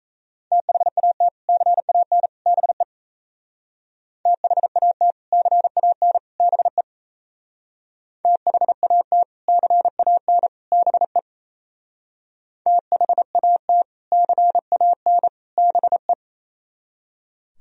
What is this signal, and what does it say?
Morse code, heard as a keyed tone, about 35 wpm, 700 Hz. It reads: THAT CAN BE THAT CAN BE THAT CAN BE THAT CAN BE
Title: Speed-Racing - Sets of 3 Words 28wpm